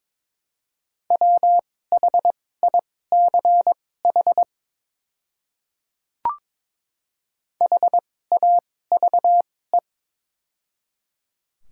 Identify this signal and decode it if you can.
Morse code, heard as a keyed tone, about 22 wpm, spaced as for 16 wpm, 700 Hz. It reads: WHICH HAVE
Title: 2x Character Spacing - Top 300 Words - 22wpm